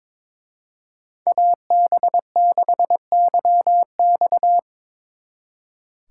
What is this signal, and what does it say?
Morse code, heard as a keyed tone, about 25 wpm, 700 Hz. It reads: AB6YX
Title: Speed Racing - USA 2x2 Call Signs 22wpm